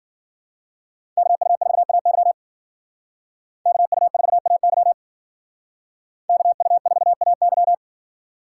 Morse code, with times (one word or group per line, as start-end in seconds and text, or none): 1.2-2.3 KU4AY
3.7-4.9 KU4AY
6.3-7.7 KU4AY